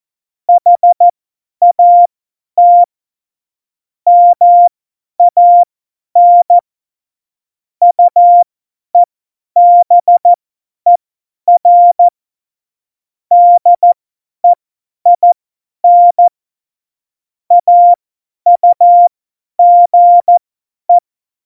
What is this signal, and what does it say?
Morse code, heard as a keyed tone, about 14 wpm, spaced as for 10 wpm, 700 Hz. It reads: HAT MAN UEBER DEIN AUGE
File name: Morsekurs / Witze / Witz 13. Februar 2018, 50 F70